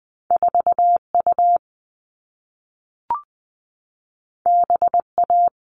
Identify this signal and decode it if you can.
Morse code, heard as a keyed tone, about 20 wpm, 700 Hz. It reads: 4U BA
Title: All Cut Numbers - 2-Digit 20wpm